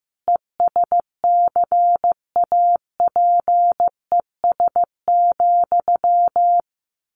Morse code, heard as a keyed tone, about 15 wpm, 700 Hz. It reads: ESCAPES,
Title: A Sound Of Thunder 15wpm